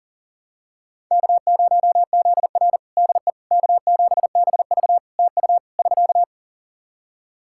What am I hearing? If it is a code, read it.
Morse code, heard as a keyed tone, about 40 wpm, 700 Hz. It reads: K0ZR DE K7BV TU <SK>
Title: Practice QSO - Exchange 4 - Ending - 40wpm